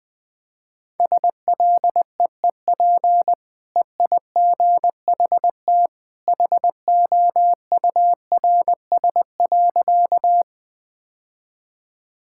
Morse code, heard as a keyed tone, about 20 wpm, 700 Hz. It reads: SLEEP EIGHT HOURS.
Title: Easy Sentences with Hints - 20wpm